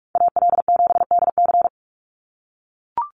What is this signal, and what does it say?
Morse code, heard as a keyed tone, about 44 wpm, 700 Hz. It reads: AL7DC
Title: USA Call Signs 45wpm